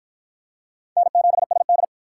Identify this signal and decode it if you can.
Morse code, heard as a keyed tone, about 50 wpm, 700 Hz. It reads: N7SD